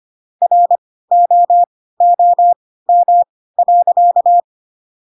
Morse code, heard as a keyed tone, about 25 wpm, spaced as for 16 wpm, 700 Hz. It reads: ROOM.
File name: Pinocchio / Pinocchio ch7 10 25 ik1hge